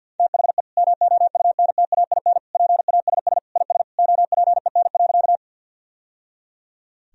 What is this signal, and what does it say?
Morse code, heard as a keyed tone, about 47 wpm, 700 Hz. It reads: THE MOUNTAIN PASS IS OPEN.
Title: Sentences from Top 1000 Words - Review Part 1 - 50wpm